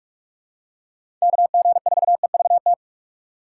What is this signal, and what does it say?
Morse code, heard as a keyed tone, about 46 wpm, 700 Hz. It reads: KG4EVT
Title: USA Call Signs - 6 Characters - 4x Character Spacing - 45wpm